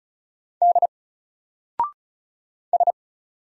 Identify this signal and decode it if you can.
Morse code, heard as a keyed tone, about 36 wpm, 700 Hz. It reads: D S
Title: Single Character (Letter:Number:Punctuation:Prosign) 35wpm